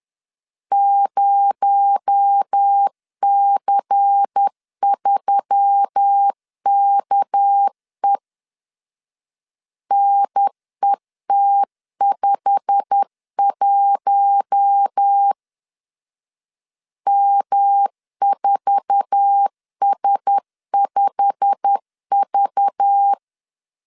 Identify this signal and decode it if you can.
Morse code, heard as a keyed tone, about 10 wpm, 800 Hz. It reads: 0C3KE NET51 M4S5V
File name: LEZIONI / 8 YJ45 / 8 YJ45 10wpm 2549